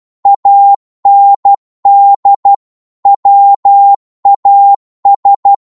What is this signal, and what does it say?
Morse code, heard as a keyed tone, about 12 wpm, 800 Hz. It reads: AND WAS